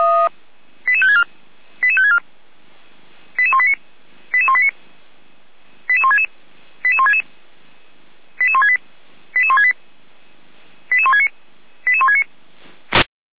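Five-tone selective call (ZVEI), heard as quick runs of five tones, tones of about 70 ms. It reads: 88564 88564 88189 88189 88166 88166 88168 88168 88169 88169
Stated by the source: ZVEI five-tone call 88355 received on an SDR.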